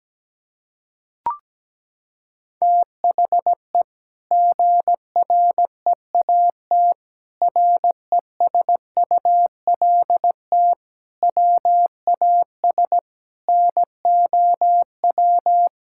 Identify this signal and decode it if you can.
Morse code, heard as a keyed tone, about 17 wpm, 700 Hz. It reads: THE GREAT RESULT WAS NOW
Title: Sentences from Top 1000 Words - Review Part 5 - 17wpm